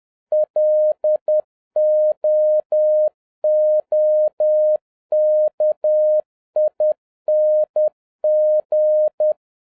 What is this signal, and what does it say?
Morse code, heard as a keyed tone, about 10 wpm, 610 Hz. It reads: LOOKING
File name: 2011-07-mp3 / n46 JOHN K3TN 10wpm